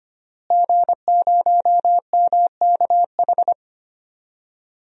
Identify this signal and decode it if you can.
Morse code, heard as a keyed tone, about 25 wpm, 700 Hz. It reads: G0MKH
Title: DXCC-100 Call Signs - 5 Characters - Repeated 2x - 25wpm